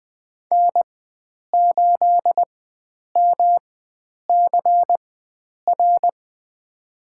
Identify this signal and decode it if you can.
Morse code, heard as a keyed tone, about 20 wpm, spaced as for 9 wpm, 700 Hz. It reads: N8MCR
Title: USA Call Signs - 5 Characters - 4x Character Spacing - 20wpm